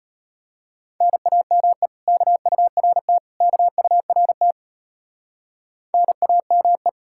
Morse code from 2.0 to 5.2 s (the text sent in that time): KURT KURT